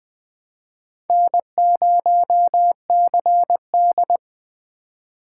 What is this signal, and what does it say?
Morse code, heard as a keyed tone, about 20 wpm, 700 Hz. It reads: N0CD
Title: USA Call Signs - 4 Characters - 3x Character Spacing - 20wpm